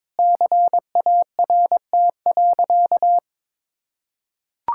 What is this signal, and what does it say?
Morse code, heard as a keyed tone, about 22 wpm, 700 Hz. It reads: CART.
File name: practice / 22wpm / Easy Sentences with Hints - 22wpm